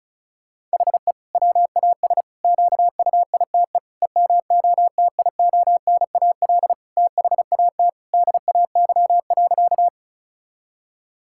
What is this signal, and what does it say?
Morse code, heard as a keyed tone, about 34 wpm, 700 Hz. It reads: HE WAS QUITE EMOTIONAL THAT DAY.
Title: Sentences from Top 1000 Words - Review Part 5 - 35wpm